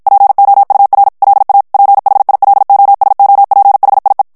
Morse code, heard as a keyed tone, about 54 wpm, 800 Hz. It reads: POUR LA PHILOSOPHIE
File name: FRANCAIS TEXTE CLAIR 2 VITESSE 50 MOTS MINUTES